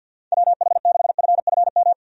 Morse code, heard as a keyed tone, about 52 wpm, 700 Hz. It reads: WH6FFM
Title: USA Call Signs - 6 Characters - 50wpm